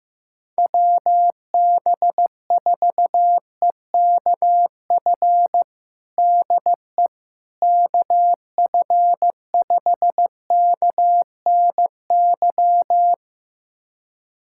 Morse code, heard as a keyed tone, about 15 wpm, 700 Hz. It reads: WB4EKF DE KF5KNY
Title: USA Callsign DE Callsign - 6 Characters - 15wpm